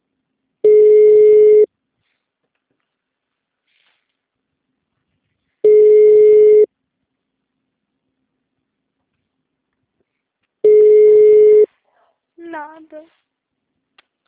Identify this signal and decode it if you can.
Ringback tone (Europe, 425 Hz), with speech.